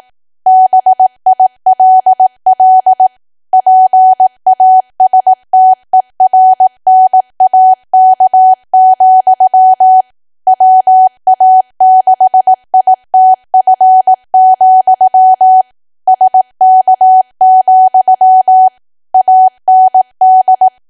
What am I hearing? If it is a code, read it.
Morse code, heard as a keyed tone, about 18 wpm, 750 Hz. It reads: BILL PASTERNAK, WA6ITF, SK, AND
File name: Archive / 18 WPM / 18WPM